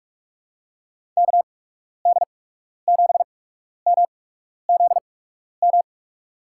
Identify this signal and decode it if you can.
Morse code, heard as a keyed tone, about 44 wpm, spaced as for 12 wpm, 700 Hz. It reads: KD7MZM
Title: USA Call Signs - 8x Character Spacing - 45wpm